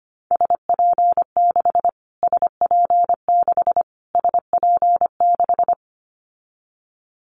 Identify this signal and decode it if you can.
Morse code, heard as a keyed tone, about 25 wpm, 700 Hz. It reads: SP6 SP6 SP6
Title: Callsign Prefixes for Top 25 DXCC - Repeated 3x - 25wpm